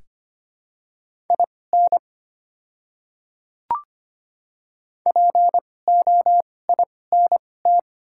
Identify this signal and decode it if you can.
Morse code, heard as a keyed tone, about 25 wpm, spaced as for 18 wpm, 700 Hz. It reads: IN POINT
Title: 2x Character Spacing - Top 200 Words - 25wpm